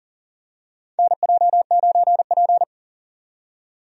Morse code, heard as a keyed tone, about 40 wpm, 700 Hz. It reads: NJ9P
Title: USA Call Signs - 4 Characters - 40wpm